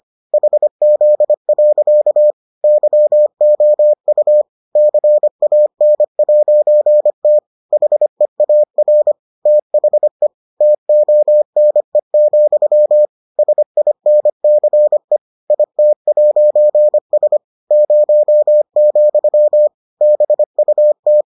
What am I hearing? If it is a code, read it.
Morse code, heard as a keyed tone, about 25 wpm, 600 Hz. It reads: HZ. YOU CAN'T HEAR THE TONE, SINCE IT'S 0, BUT